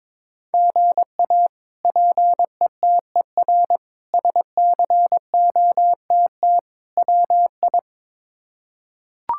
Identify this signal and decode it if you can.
Morse code, heard as a keyed tone, about 22 wpm, 700 Hz. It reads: GA PETER SCOTT WI
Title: K1USN SST QSO Practice - 22wpm